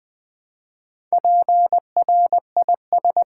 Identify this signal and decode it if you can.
Morse code, heard as a keyed tone, about 21 wpm, 700 Hz. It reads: PRIS
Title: Build A Word - 2 Letters At A Time - 6-Letter Words - 20wpm